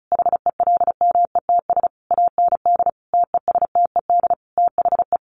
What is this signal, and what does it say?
Morse code, heard as a keyed tone, about 34 wpm, 700 Hz. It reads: HELMETS AND TESTED THE
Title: A Sound Of Thunder 35wpm